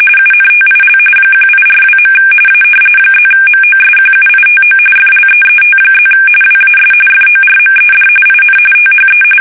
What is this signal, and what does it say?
RTTY, about 45.45 baud. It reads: UNTIL 1842. A +MOSQUITO FLEET+ WAS FORMED IN THE
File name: fsk850 / usnavy2